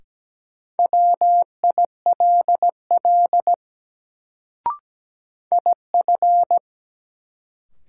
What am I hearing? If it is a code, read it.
Morse code, heard as a keyed tone, about 17 wpm, 700 Hz. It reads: WILL IF
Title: Sets of 2 Words - New Word Drill - 17wpm